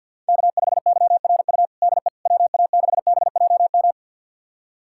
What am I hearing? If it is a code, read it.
Morse code, heard as a keyed tone, about 52 wpm, 700 Hz. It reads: K5YRU DE WA6BJM
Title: USA Callsign DE Callsign 50wpm